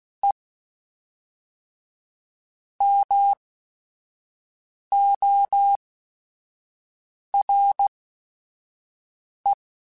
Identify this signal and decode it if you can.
Morse code, heard as a keyed tone, about 16 wpm, spaced as for 5 wpm, 800 Hz. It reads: E MORE
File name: Technical Subjects / Technical Subjects-mp3s / XL @5